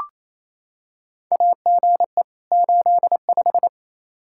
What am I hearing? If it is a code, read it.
Morse code, heard as a keyed tone, about 27 wpm, 700 Hz. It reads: AGE 85
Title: Practice QSO - Exchange 3 - More Info - 28wpm